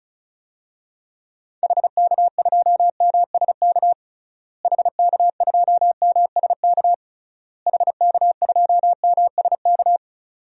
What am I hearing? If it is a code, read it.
Morse code, heard as a keyed tone, about 36 wpm, 700 Hz. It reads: HK2MSK HK2MSK HK2MSK
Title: DXCC-100 Call Signs - 6 Characters - Repeated 3x - 35wpm